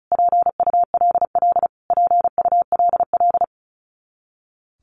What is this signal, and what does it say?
Morse code, heard as a keyed tone, about 34 wpm, 700 Hz. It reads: PULL PULL
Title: Top 500 Words Repeated 35wpm